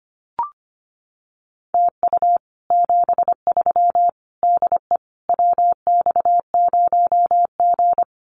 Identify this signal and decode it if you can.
Morse code, heard as a keyed tone, about 25 wpm, 700 Hz. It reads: TU 73 DE WX0G
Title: Common QSO Phrases Encore 25wpm